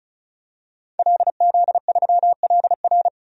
Morse code, heard as a keyed tone, about 36 wpm, 700 Hz. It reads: LZ3LR
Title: DXCC-100 Call Signs - 5 Characters - 35wpm